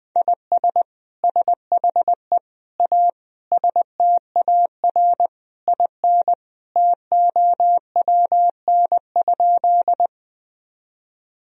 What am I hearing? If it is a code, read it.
Morse code, heard as a keyed tone, about 20 wpm, 700 Hz. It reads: IS SHE A STAR IN TOWN?
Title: Sentences from Top 1000 Words - Review Part 2 - 20wpm